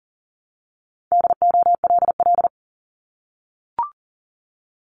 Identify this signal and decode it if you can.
Morse code, heard as a keyed tone, about 40 wpm, 700 Hz. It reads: DOLL
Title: Multi-letter Phonemes Repeated 3x 40wpm